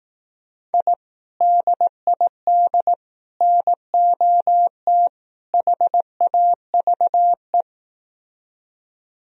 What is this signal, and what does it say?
Morse code, heard as a keyed tone, about 18 wpm, 700 Hz. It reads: I DID NOT HAVE
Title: Speed-Racing - Sets of 4 Words 15wpm